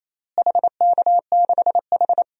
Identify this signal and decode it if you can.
Morse code, heard as a keyed tone, about 28 wpm, 700 Hz. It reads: HK6H
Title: DXCC-100 Call Signs - 4 Characters - Repeated 2x - 28wpm